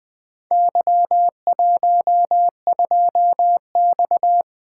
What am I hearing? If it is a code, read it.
Morse code, heard as a keyed tone, about 20 wpm, 700 Hz. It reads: Y12X
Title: Four Letter-Number Combinations - 20wpm